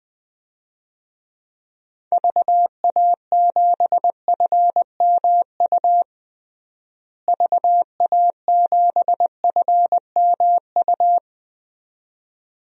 Morse code, heard as a keyed tone, about 20 wpm, 700 Hz. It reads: VA7FMU VA7FMU
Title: DXCC-100 Call Signs - 6 Characters - Repeated 2x - 20wpm